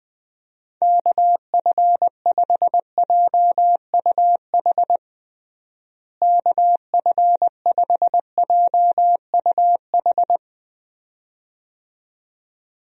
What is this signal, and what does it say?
Morse code, heard as a keyed tone, about 20 wpm, 700 Hz. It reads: KF5JUH KF5JUH
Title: USA Call Signs - 6 Characters - Repeated 2x - 20wpm